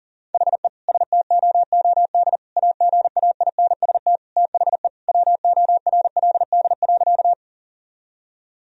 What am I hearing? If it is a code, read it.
Morse code, heard as a keyed tone, about 37 wpm, 700 Hz. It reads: HE STOOD AGAINST THE WORLD.
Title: Sentences from Top 1000 Words - Review Part 2 - 40wpm